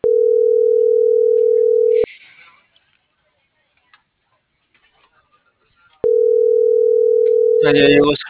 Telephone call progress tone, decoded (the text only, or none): ringback tone